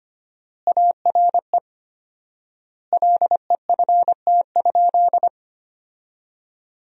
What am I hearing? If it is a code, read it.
Morse code, heard as a keyed tone, about 25 wpm, 700 Hz. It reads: ARE LEFT?
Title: 4x Word-Spacing - Sentences from Top 300 Words - 25wpm